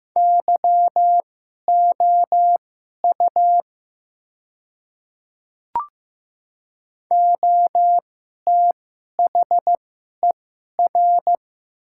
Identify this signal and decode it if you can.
Morse code, heard as a keyed tone, about 15 wpm, spaced as for 11 wpm, 700 Hz. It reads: YOU OTHER
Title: 2x Character Spacing - Top 100 Words - 15wpm